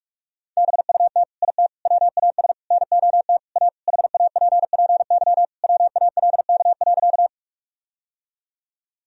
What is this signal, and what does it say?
Morse code, heard as a keyed tone, about 41 wpm, 700 Hz. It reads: BUT IT WAS NOT A HAPPY WALK.
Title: Sentences from Top 1000 Words - Review Part 1 - 45wpm